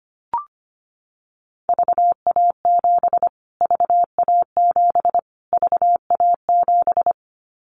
Morse code, heard as a keyed tone, about 25 wpm, 700 Hz. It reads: VA7 VA7 VA7
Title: Callsign Prefixes for Top 25 DXCC - Repeated 3x - 25wpm